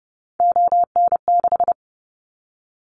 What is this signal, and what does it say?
Morse code, heard as a keyed tone, about 30 wpm, 700 Hz. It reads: ON6
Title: Callsign Prefixes for Top 100 DXCC - 30wpm